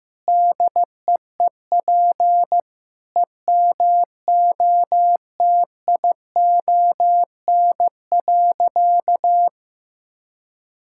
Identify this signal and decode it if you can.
Morse code, heard as a keyed tone, about 15 wpm, 700 Hz. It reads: DEEP EMOTION.
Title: Sentences from Top 1600 Words 15wpm